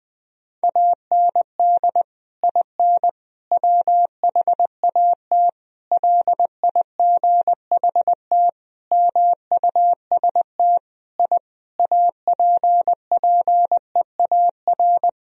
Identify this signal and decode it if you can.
Morse code, heard as a keyed tone, about 20 wpm, 700 Hz. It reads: AND IN WHAT LIGHT MUST I APPEAR
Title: Sentences from Top 1000 Words - Review Part 2 - 20wpm